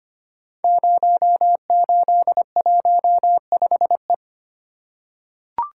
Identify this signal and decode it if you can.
Morse code, heard as a keyed tone, about 25 wpm, 700 Hz. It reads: 0815E
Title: Code Groups - 5 Letter:Numbers - 25wpm